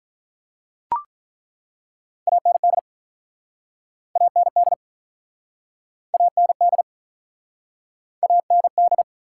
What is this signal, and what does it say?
Morse code, heard as a keyed tone, about 48 wpm, 700 Hz. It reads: AND AND AND AND